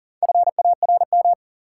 Morse code, heard as a keyed tone, about 40 wpm, 700 Hz. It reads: FARM
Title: Sentences from Top 1000 Words - Review Part 2 - New Word Drill - 40wpm